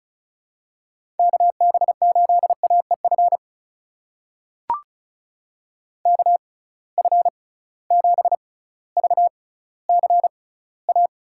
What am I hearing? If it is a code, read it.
Morse code, heard as a keyed tone, about 36 wpm, spaced as for 18 wpm, 700 Hz. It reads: KB8AEF KF7VCA
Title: USA Call Signs - 6 Characters - 6x Character Spacing - 35wpm